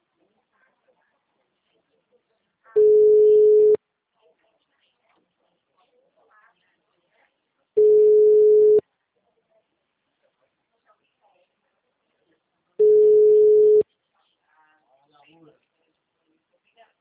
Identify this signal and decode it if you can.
Ringback tone (Europe, 425 Hz), with speech.
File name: out- -2009- - - .30870